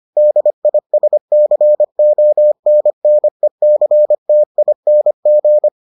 Morse code, heard as a keyed tone, about 25 wpm, 600 Hz. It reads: DISCONNECTING